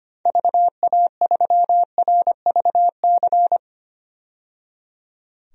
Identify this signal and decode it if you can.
Morse code, heard as a keyed tone, about 25 wpm, 700 Hz. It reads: VA3RVC
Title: DXCC-100 Call Signs - 6 Characters - 25wpm